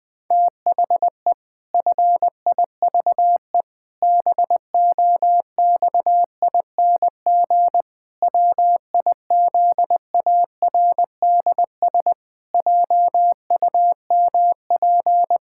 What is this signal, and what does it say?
Morse code, heard as a keyed tone, about 20 wpm, 700 Hz. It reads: THE FIVE BOXING WIZARDS JUMP
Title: Pangram Challenge v2 - 20wpm